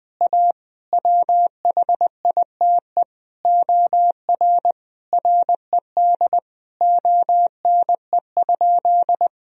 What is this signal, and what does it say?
Morse code, heard as a keyed tone, about 21 wpm, 700 Hz. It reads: A WHITE OR RED ONE?
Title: Sentences from Top 1000 Words - Review Part 2 - 20wpm